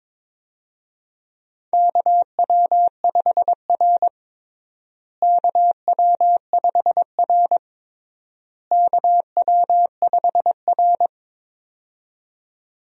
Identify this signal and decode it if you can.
Morse code, heard as a keyed tone, about 22 wpm, 700 Hz. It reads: KW5R KW5R KW5R
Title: USA Call Signs - 4 Characters - Repeated 3x - 22wpm